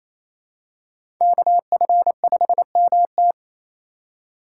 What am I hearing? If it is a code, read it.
Morse code, heard as a keyed tone, about 28 wpm, 700 Hz. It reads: KF5MT
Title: USA Call Signs - 5 Characters - 2x Character Spacing - 28wpm